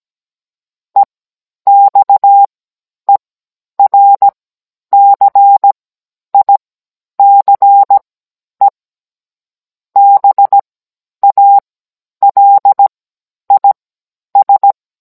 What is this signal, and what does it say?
Morse code, heard as a keyed tone, about 17 wpm, spaced as for 10 wpm, 800 Hz. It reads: EXERCICE BALIS
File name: Cours-NR18-16mm